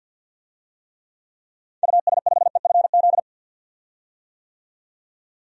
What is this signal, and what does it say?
Morse code, heard as a keyed tone, about 51 wpm, 700 Hz. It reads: US5EFZ
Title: DXCC-100 Call Signs - 6 Characters - 50wpm